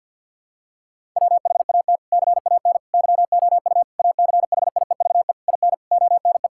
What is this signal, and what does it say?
Morse code, heard as a keyed tone, about 52 wpm, 700 Hz. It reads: WHAT CAN YOU ACHIEVE IN ONE
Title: Sentences from Top 1000 Words - Review Part 6 - 50wpm